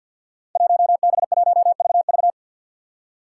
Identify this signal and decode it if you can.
Morse code, heard as a keyed tone, about 52 wpm, 700 Hz. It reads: 1B1VV
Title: Five Challenging Letter-Number Combinations 50wpm